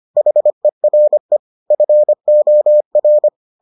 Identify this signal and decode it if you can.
Morse code, heard as a keyed tone, about 25 wpm, 600 Hz. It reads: HERE FOR